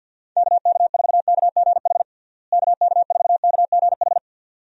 Morse code, heard as a keyed tone, about 50 wpm, 700 Hz. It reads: KK4KGH KK4KGH
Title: USA Call Signs - 6 Characters - Repeated 2x - 50wpm